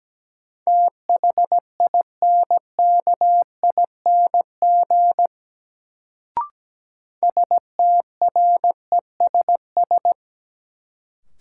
Morse code, heard as a keyed tone, about 17 wpm, 700 Hz. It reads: THINKING STRESS
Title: Sentences from Top 1000 Words - Review Part 6 - New Word Drill - 17wpm